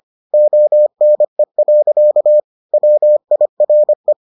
Morse code, heard as a keyed tone, about 25 wpm, 600 Hz. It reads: ONE. WIRE